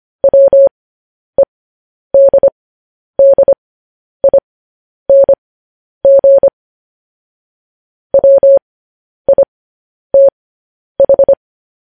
Morse code, heard as a keyed tone, about 25 wpm, spaced as for 10 wpm, 550 Hz. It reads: WEDDING WITH
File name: podcast / News headlines-1x-25x10